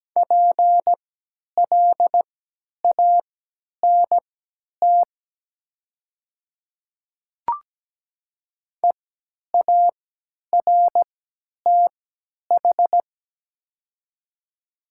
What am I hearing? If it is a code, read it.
Morse code, heard as a keyed tone, about 17 wpm, spaced as for 10 wpm, 700 Hz. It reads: PLANT EARTH
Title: 3x Character Spacing - Top 300 Words - 17wpm